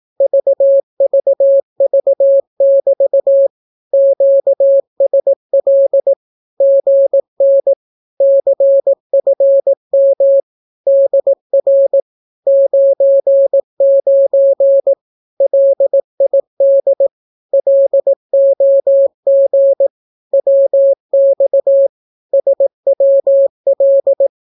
Morse code, heard as a keyed tone, about 18 wpm, 550 Hz. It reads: VVV<BT> QSL GN CFM DR 99 LID LOG WX SWL
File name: naukapodstaw / skroty / skroty